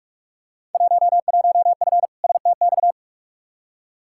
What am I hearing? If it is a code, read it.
Morse code, heard as a keyed tone, about 47 wpm, 700 Hz. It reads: 11F STX
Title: Field Day Exchanges Encore - 45wpm